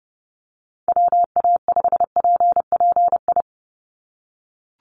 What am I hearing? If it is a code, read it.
Morse code, heard as a keyed tone, about 30 wpm, 700 Hz. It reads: WA5PPI
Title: USA Call Signs 30wpm